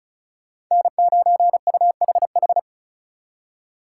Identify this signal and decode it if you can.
Morse code, heard as a keyed tone, about 43 wpm, 700 Hz. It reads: N9UHH
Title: Speed Racing - USA Call Signs 35wpm